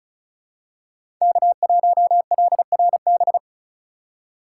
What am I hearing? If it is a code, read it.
Morse code, heard as a keyed tone, about 36 wpm, 700 Hz. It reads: K1LRB
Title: USA Call Signs - 5 Characters - 3x Character Spacing - 35wpm